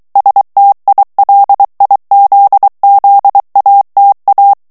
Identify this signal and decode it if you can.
Morse code, heard as a keyed tone, about 23 wpm, 800 Hz. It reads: STILIZZATA
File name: morse mp3 / ITALIEN texte clair 1 vitesse 22 mots minute